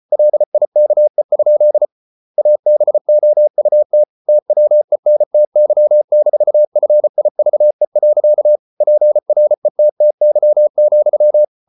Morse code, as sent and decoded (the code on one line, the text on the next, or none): .-.. .. -.- . ..--.. / .- -... --- ..- - / - .-- . -. - -.-- -....- ..-. .. ...- . .-.-.- / .--. .-. - - -.-- --..--
LIKE? ABOUT TWENTY-FIVE. PRETTY,